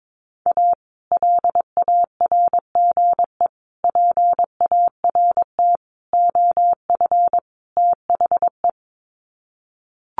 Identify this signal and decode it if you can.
Morse code, heard as a keyed tone, about 22 wpm, 700 Hz. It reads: A LARGE PART OF THE
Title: Sets of 5 Words Encore2 22wpm